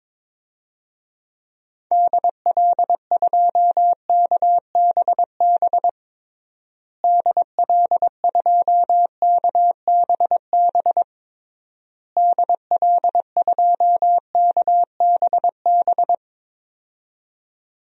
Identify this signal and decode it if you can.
Morse code, heard as a keyed tone, about 22 wpm, 700 Hz. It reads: DL2KBB DL2KBB DL2KBB
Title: DXCC-100 Call Signs - 6 Characters - Repeated 3x - 22wpm